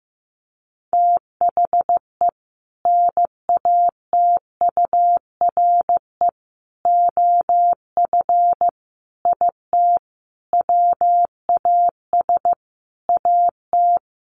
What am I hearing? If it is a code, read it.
Morse code, heard as a keyed tone, about 15 wpm, 700 Hz. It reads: THE NATURE OF IT WAS AT
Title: Sentences from Top 1400 Words 15wpm